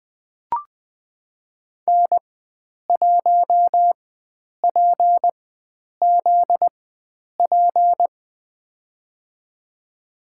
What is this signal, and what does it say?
Morse code, heard as a keyed tone, about 21 wpm, spaced as for 9 wpm, 700 Hz. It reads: N1PZP
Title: USA Call Signs - 5 Characters - 4x Character Spacing - 20wpm